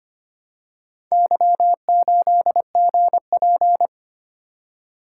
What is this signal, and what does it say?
Morse code, heard as a keyed tone, about 25 wpm, 690 Hz. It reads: Y8GP
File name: practice / 25wpm / Four Letter-Number Combinations - 25wpm